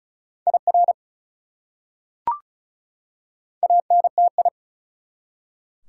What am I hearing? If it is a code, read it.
Morse code, heard as a keyed tone, about 34 wpm, 700 Hz. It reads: IR ANTI
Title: Prefixes and Suffixes 35wpm